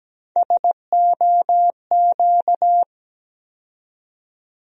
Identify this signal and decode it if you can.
Morse code, heard as a keyed tone, about 17 wpm, 700 Hz. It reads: SOQ
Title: USA Call Sign Postfixes - 3 Letters 17wpm